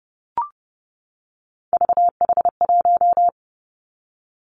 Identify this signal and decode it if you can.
Morse code, heard as a keyed tone, about 30 wpm, 700 Hz. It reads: VH1
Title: Three Challenging Letter-Number Combinations 30wpm